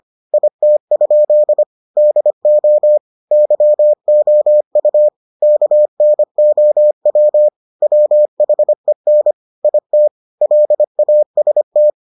Morse code, heard as a keyed tone, about 25 wpm, 600 Hz. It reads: IT? DO YOU KNOW WHEN IT LAST